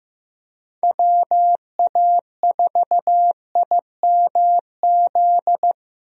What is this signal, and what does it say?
Morse code, heard as a keyed tone, about 16 wpm, 700 Hz. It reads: WA4IMZ
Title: USA Call Signs - 6 Characters - Repeated 3x - 15wpm